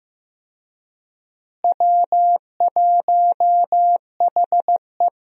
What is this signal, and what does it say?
Morse code, heard as a keyed tone, about 16 wpm, 700 Hz. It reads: W1HE
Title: USA Call Signs - 4 Characters - Repeated 3x - 15wpm